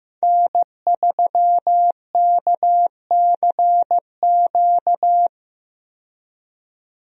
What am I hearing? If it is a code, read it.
Morse code, heard as a keyed tone, about 15 wpm, 700 Hz. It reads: N3KCQ
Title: USA Call Signs - 5 Characters - 15wpm